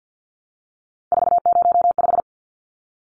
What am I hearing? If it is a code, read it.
Morse code, heard as a keyed tone, about 50 wpm, 700 Hz. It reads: 405
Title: Numbers - 3-Digit 50wpm